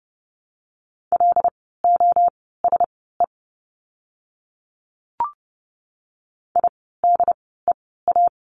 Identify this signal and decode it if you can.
Morse code, heard as a keyed tone, about 30 wpm, spaced as for 17 wpm, 700 Hz. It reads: LOSE IDEA
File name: course / 30:10wpm / 108 Lesson 18f - Letter m - Words 30wpm